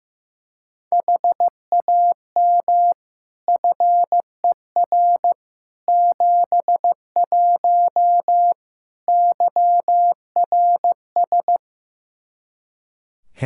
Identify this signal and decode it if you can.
Morse code, heard as a keyed tone, about 15 wpm, 700 Hz. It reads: HAM FER 71 YRS
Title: Practice QSO - Exchange 3 - More Info - 15wpm